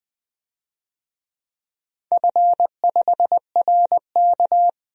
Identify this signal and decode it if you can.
Morse code, heard as a keyed tone, about 20 wpm, 700 Hz. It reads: F5RK
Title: DXCC-100 Call Signs - 4 Characters - 20wpm